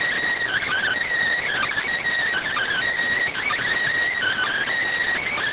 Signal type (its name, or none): Emergency Alert System (SAME) tones